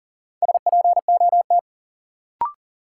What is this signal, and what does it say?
Morse code, heard as a keyed tone, about 40 wpm, 700 Hz. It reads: SPOT
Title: Sentences from Top 1100 Words - New Word Drill Alt - 40wpm